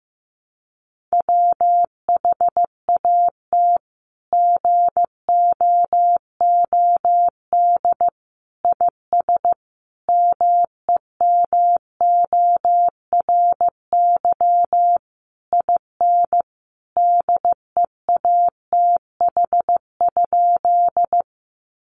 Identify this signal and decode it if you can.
Morse code, heard as a keyed tone, about 15 wpm, 700 Hz. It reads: WHAT GOOD IS MEMORY IN DEATH?
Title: Sentences from Top 1300 Words 15wpm